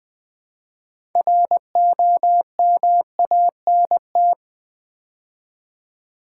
Morse code, heard as a keyed tone, about 21 wpm, 700 Hz. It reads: ROMANT
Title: Build A Word - 2 Letters At A Time - 8-Letter Words - 20wpm